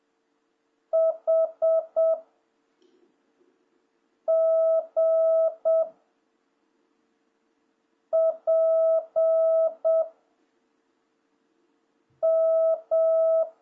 Morse code, heard as a keyed tone, about 7 wpm, 660 Hz. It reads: H G P M
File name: jh8cbh / 2026cw